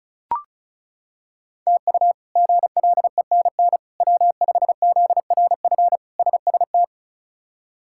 Morse code, heard as a keyed tone, about 36 wpm, 700 Hz. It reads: TU GLENN W5ZRF SST
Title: K1USN SST QSO Practice - 35wpm